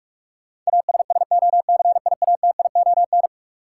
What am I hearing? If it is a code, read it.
Morse code, heard as a keyed tone, about 42 wpm, 700 Hz. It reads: ASSOCIATION
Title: Sentences from Top 1100 Words - New Word Drill Alt - 45wpm